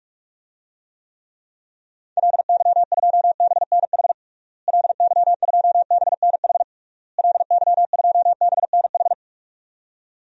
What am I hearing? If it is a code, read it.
Morse code, heard as a keyed tone, about 46 wpm, 700 Hz. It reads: LY2BNH LY2BNH LY2BNH
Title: DXCC-100 Call Signs - 6 Characters - Repeated 3x - 45wpm